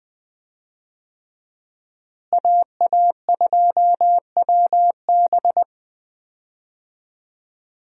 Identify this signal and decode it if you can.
Morse code, heard as a keyed tone, about 20 wpm, 700 Hz. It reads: AA2WB
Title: USA Call Signs - 5 Characters - Repeated 3x - 20wpm